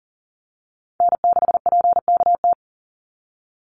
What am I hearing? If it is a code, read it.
Morse code, heard as a keyed tone, about 40 wpm, 700 Hz. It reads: N6PKT
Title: NA SOTA Chaser Callsigns 40wpm